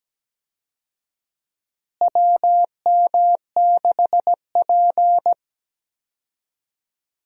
Morse code, heard as a keyed tone, about 17 wpm, 700 Hz. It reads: WM6P
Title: USA Call Signs - 4 Characters - Repeated 3x - 17wpm